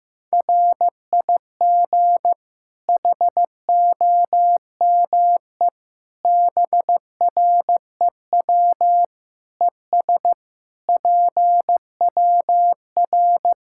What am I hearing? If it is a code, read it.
Morse code, heard as a keyed tone, about 16 wpm, 700 Hz. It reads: RIG HOME BREW ES PWR
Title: K1USN SST QSO Practice 15wpm